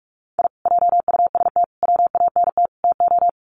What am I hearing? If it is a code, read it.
Morse code, heard as a keyed tone, about 44 wpm, 700 Hz. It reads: I JUST WANT TO
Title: Sets of 4 Words 45wpm